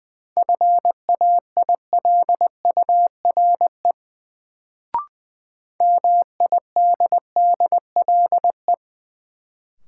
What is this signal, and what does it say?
Morse code, heard as a keyed tone, about 20 wpm, 700 Hz. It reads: FAILURE MIDDLE
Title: Sentences from Top 900 Words - New Word Drill - 20wpm